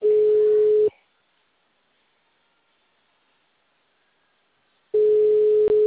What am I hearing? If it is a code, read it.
Ringback tone (Europe, 425 Hz).
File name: out- -2006- - - .4614